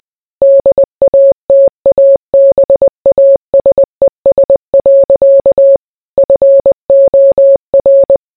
Morse code, heard as a keyed tone, about 20 wpm, 550 Hz. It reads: DATABASES. FOR